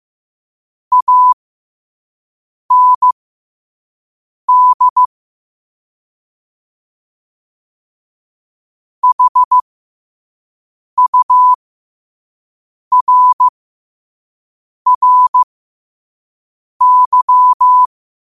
Morse code, heard as a keyed tone, about 15 wpm, spaced as for 5 wpm, 1000 Hz. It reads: AND HURRY
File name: qotdfiles / 5wpm